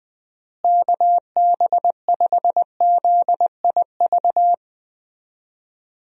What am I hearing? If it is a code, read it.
Morse code, heard as a keyed tone, about 21 wpm, 700 Hz. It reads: KB5ZIV
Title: USA Call Signs - 6 Characters - Repeated 3x - 20wpm